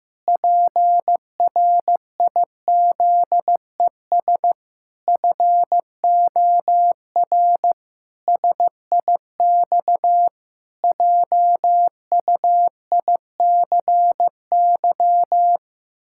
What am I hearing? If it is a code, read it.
Morse code, heard as a keyed tone, about 16 wpm, 700 Hz. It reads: PRIZES FOR SIX JUICY
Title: Pangram Challenge v2 - 15wpm